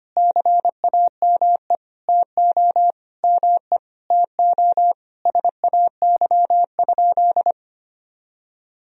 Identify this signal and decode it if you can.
Morse code, heard as a keyed tone, about 25 wpm, 700 Hz. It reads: CAME TO ME TO SAY?
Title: Sentences from Top 200 Words and Abbreviations 25wpm